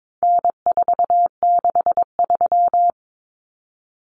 Morse code, heard as a keyed tone, about 22 wpm, 700 Hz. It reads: N463
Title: Common Cut Numbers - 4-Digit 22wpm